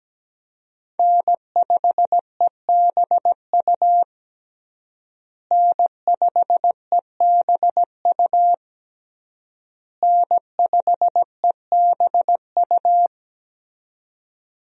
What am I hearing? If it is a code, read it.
Morse code, heard as a keyed tone, about 17 wpm, 700 Hz. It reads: N5EBU N5EBU N5EBU
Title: USA Call Signs - 5 Characters - Repeated 3x - 17wpm